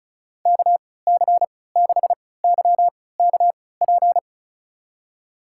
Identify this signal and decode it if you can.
Morse code, heard as a keyed tone, about 34 wpm, spaced as for 20 wpm, 700 Hz. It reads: KC6YKP
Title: USA Call Signs - 6 Characters - 3x Character Spacing - 35wpm